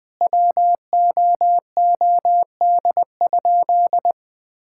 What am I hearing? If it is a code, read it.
Morse code, heard as a keyed tone, about 21 wpm, 700 Hz. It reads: WOOD?
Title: Sentences from Top 1600 Words 20wpm